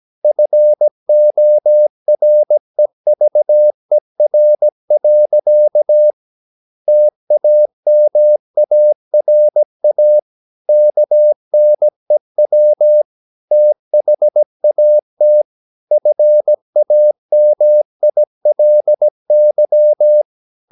Morse code, heard as a keyed tone, about 17 wpm, 600 Hz. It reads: FOREVER. TAMARA KNEW THAT FAMILY
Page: Romanen Never af Ken Follet på engelsk.